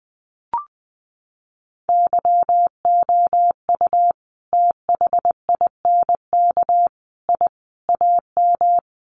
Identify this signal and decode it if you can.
Morse code, heard as a keyed tone, about 20 wpm, 700 Hz. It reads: YOU THINK I AM
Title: Sentences from Top 1200 Words 20wpm